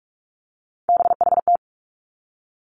Morse code, heard as a keyed tone, about 46 wpm, 700 Hz. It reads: BHT
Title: Three Challenging Letter-Number Combinations 45wpm